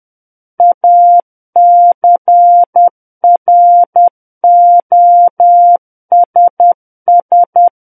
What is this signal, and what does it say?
Morse code, heard as a keyed tone, about 10 wpm, 700 Hz. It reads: ACROSS
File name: Book One / 01-Chapter 1 - 10 WPM